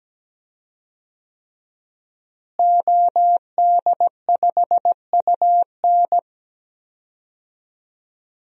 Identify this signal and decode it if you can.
Morse code, heard as a keyed tone, about 17 wpm, 700 Hz. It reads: OD5UN
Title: DXCC-100 Call Signs - 5 Characters - Repeated 3x - 17wpm